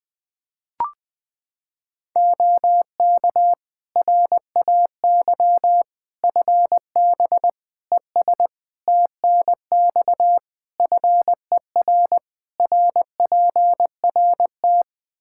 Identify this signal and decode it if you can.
Morse code, heard as a keyed tone, about 20 wpm, 700 Hz. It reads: OK RAY FB ES TNX FER RPRT
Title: Practice QSO - Exchange 2 - Info - 20wpm